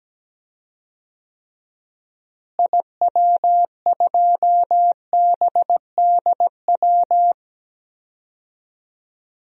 Morse code, heard as a keyed tone, about 17 wpm, 700 Hz. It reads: IW2BDW
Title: DXCC-100 Call Signs - 6 Characters - Repeated 3x - 17wpm